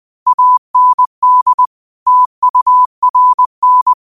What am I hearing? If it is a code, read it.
Morse code, heard as a keyed tone, about 20 wpm, 1000 Hz. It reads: AND TURN
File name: qotdfiles / 20wpm